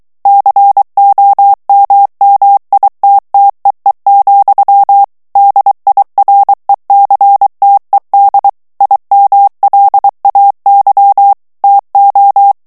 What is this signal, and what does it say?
Morse code, heard as a keyed tone, about 23 wpm, 800 Hz. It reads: COMMITTEE, DIRECTED IMLAY TO
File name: ANGLAIS texte clair 3 vitesse 22 mots minute